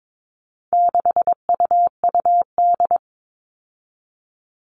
Morse code, heard as a keyed tone, about 22 wpm, 700 Hz. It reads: 6UUD
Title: All Cut Numbers - 4-Digit 22wpm